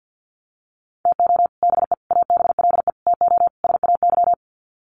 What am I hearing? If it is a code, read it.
Morse code, heard as a keyed tone, about 50 wpm, 700 Hz. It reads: TO BE ABLE TO SAY
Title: Sets of 5 Words Encore2 50wpm